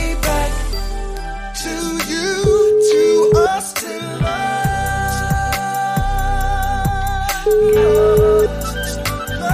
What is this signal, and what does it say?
Ringback tone (Europe, 425 Hz), with noise.